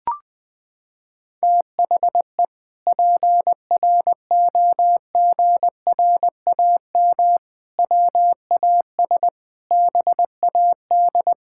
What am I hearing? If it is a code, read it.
Morse code, heard as a keyed tone, about 20 wpm, 700 Hz. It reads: THE PROGRAM WAS BAD
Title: Sentences from Top 1000 Words - Review Part 5 - 20wpm